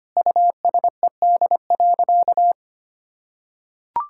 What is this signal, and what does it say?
Morse code, heard as a keyed tone, about 25 wpm, 700 Hz. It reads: USED.
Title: Sentences from Top 1600 Words 25wpm